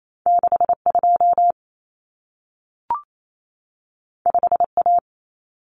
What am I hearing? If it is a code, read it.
Morse code, heard as a keyed tone, about 28 wpm, 700 Hz. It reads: 62 5A
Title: Common Cut Numbers - 2-Digit 28wpm